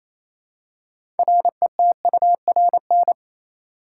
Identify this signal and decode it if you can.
Morse code, heard as a keyed tone, about 28 wpm, 700 Hz. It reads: RETURN
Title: Six Letter Words - 28wpm